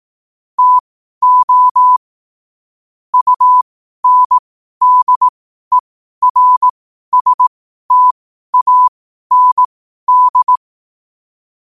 Morse code, heard as a keyed tone, about 18 wpm, spaced as for 12 wpm, 1000 Hz. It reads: TO UNDERSTAND
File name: qotdfiles / 12wpm